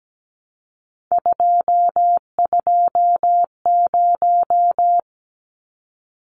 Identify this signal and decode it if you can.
Morse code, heard as a keyed tone, about 17 wpm, 700 Hz. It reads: 220
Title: Numbers - 3-Digit 17wpm